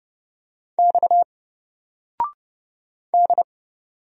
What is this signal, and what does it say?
Morse code, heard as a keyed tone, about 30 wpm, 700 Hz. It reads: X D
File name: Single-Letter-Number-30wpm